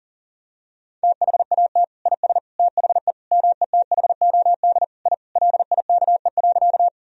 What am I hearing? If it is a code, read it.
Morse code, heard as a keyed tone, about 40 wpm, 700 Hz. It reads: THAT IS THE METHOD I LIKE.
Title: Sentences from Top 1000 Words - Review Part 3 - 40wpm